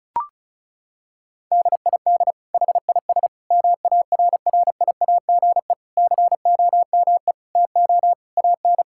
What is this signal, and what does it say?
Morse code, heard as a keyed tone, about 36 wpm, 700 Hz. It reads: DID HIS MARRIAGE COME TO AN
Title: Sentences from Top 1000 Words - Review Part 5 - 35wpm